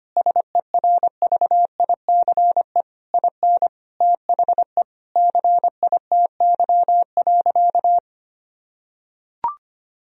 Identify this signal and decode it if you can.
Morse code, heard as a keyed tone, about 25 wpm, 700 Hz. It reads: SERVICE IN THE CITY.
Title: Sentences from Top 1000 Words - Review Part 4 - 25wpm